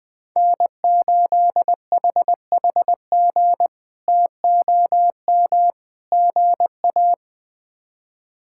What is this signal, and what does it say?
Morse code, heard as a keyed tone, about 21 wpm, 700 Hz. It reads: N8HHG TOM GA
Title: K1USN SST QSO Practice - 20wpm